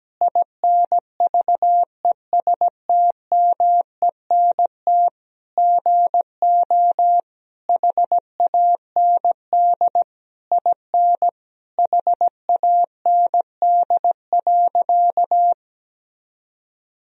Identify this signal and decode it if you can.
Morse code, heard as a keyed tone, about 17 wpm, 700 Hz. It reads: INVESTMENT GO HAND IN HAND.
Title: Sentences from Top 1000 Words - Review Part 4 - 17wpm